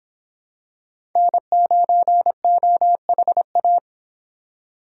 Morse code, heard as a keyed tone, about 26 wpm, 700 Hz. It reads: N9OHA